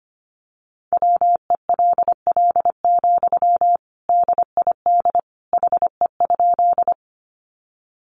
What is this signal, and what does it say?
Morse code, heard as a keyed tone, about 25 wpm, 700 Hz. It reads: WELL, DID HE?
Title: Sentences from Top 200 Words 25wpm